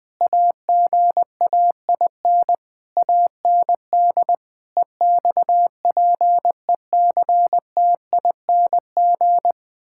Morse code, heard as a keyed tone, about 20 wpm, 700 Hz. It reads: AGAIN AND EXPECTING
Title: Sentences with 9-Word Phrases - 20wpm